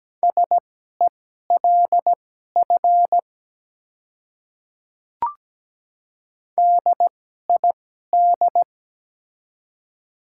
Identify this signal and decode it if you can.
Morse code, heard as a keyed tone, about 17 wpm, spaced as for 12 wpm, 700 Hz. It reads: SELF DID
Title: 2x Character Spacing - Top 200 Words - 17wpm